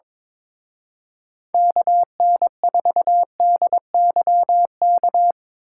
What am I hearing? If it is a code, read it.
Morse code, heard as a keyed tone, about 24 wpm, 700 Hz. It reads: KN4DYK
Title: Speed Racing - USA Call Signs 22wpm